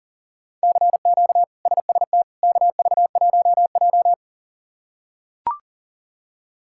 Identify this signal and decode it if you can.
Morse code, heard as a keyed tone, about 37 wpm, 700 Hz. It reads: CQ SST KV1J
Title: K1USN SST QSO Practice - 40wpm